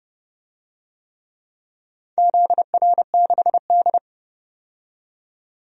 Morse code, heard as a keyed tone, about 31 wpm, 700 Hz. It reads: ZR6D
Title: DXCC-100 Call Signs - 4 Characters - 30wpm